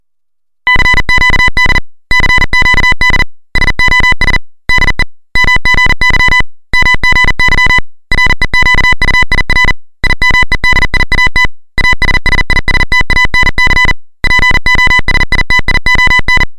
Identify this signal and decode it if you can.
Morse code, heard as a keyed tone, about 40 wpm, 2000 Hz, with noise. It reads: CQD CQD SOS DE MGY MGY REQUIR IMEDIAT ASSISTANC POSITION
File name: luxorion / Radio / titanic-40